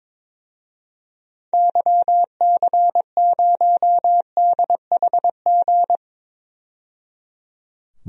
Morse code, heard as a keyed tone, about 21 wpm, 700 Hz. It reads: YC0DHG
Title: DXCC-100 Call Signs - 6 Characters - 22wpm